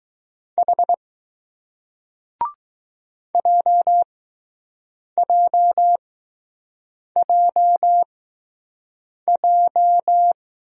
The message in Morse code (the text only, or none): H JJJJ